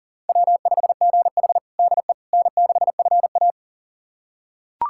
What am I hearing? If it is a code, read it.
Morse code, heard as a keyed tone, about 40 wpm, 700 Hz. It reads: W5GH DE N6FA
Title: USA Callsign DE Callsign - 4 Characters - 40wpm